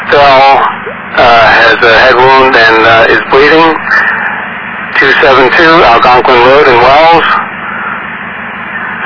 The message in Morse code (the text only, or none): none